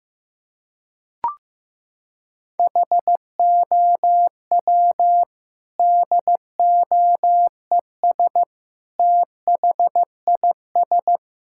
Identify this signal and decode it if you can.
Morse code, heard as a keyed tone, about 15 wpm, 700 Hz. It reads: HOW DOES THIS
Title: Sentences from Top 1000 Words - Review Part 5 - 15wpm